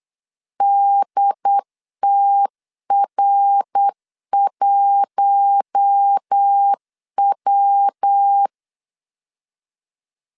Morse code, heard as a keyed tone, about 8 wpm, 800 Hz. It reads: DTR1W
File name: LEZIONI / 7 GF9Z / 7 GF9 8wpm 3545